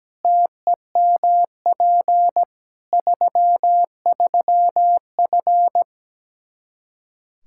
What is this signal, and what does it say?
Morse code, heard as a keyed tone, about 17 wpm, 700 Hz. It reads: TEMP 33F
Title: Practice QSO - Exchange 2 - Info - 17wpm